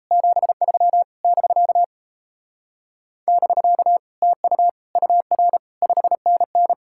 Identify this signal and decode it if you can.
Morse code, heard as a keyed tone, about 36 wpm, 700 Hz. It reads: 73 <BK> <BK> TU UR 5NN